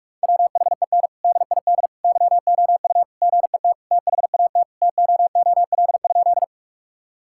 Morse code, heard as a keyed tone, about 46 wpm, 700 Hz. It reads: WHEN DID YOU GET THAT TOOL?
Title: Sentences from Top 1000 Words - Review Part 6 - 45wpm